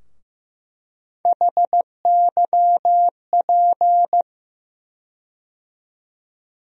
Morse code, heard as a keyed tone, about 15 wpm, 700 Hz. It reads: HYP
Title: USA Call Sign Postfixes - 3 Letters 15wpm